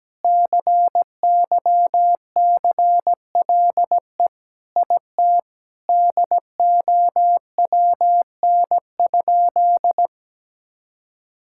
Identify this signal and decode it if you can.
Morse code, heard as a keyed tone, about 17 wpm, 700 Hz. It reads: CYCLE IT DOWN?
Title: Sentences from Top 1000 Words - Review Part 6 - 17wpm